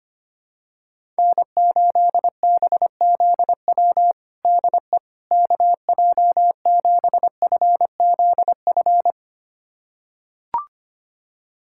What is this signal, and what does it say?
Morse code, heard as a keyed tone, about 25 wpm, 700 Hz. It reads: N8BZW DE KJ7FZF
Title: USA Callsign DE Callsign 25wpm